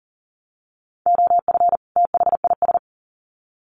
Morse code, heard as a keyed tone, about 40 wpm, 700 Hz. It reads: OF THIS
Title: Sets of 2 Words 40wpm